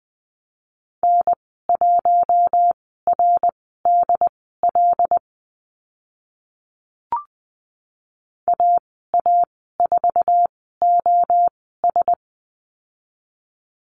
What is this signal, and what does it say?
Morse code, heard as a keyed tone, about 20 wpm, spaced as for 14 wpm, 700 Hz. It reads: N1RDL AA4OS
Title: 076 Lesson 13g - Letter L - Call Signs 20wpm